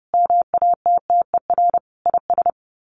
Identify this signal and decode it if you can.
Morse code, heard as a keyed tone, about 30 wpm, 700 Hz. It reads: MATTER IS
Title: Sets of 5 Words 30wpm